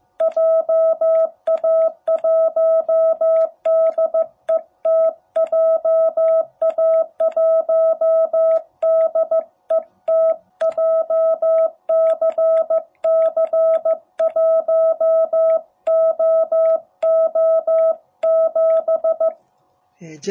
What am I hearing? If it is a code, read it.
Morse code, heard as a keyed tone, about 15 wpm, 660 Hz. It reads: JA1DET JA1DET JCC1OO7
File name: jh8cbh / 2021cw